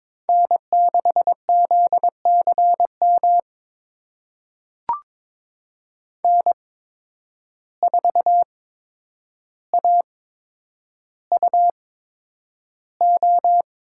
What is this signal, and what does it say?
Morse code, heard as a keyed tone, about 22 wpm, spaced as for 9 wpm, 700 Hz. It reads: N6ZCM N4AUO
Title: USA Call Signs - 5 Characters - 8x Character Spacing - 22wpm